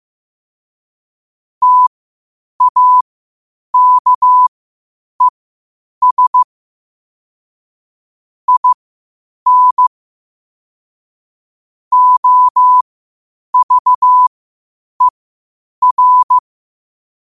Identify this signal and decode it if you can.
Morse code, heard as a keyed tone, about 15 wpm, spaced as for 8 wpm, 1000 Hz. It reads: TAKES IN OVER
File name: qotdfiles / 8wpm